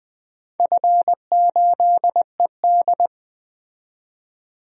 Morse code, heard as a keyed tone, about 20 wpm, 700 Hz. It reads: F8ED
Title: DXCC-100 Call Signs - 4 Characters - 20wpm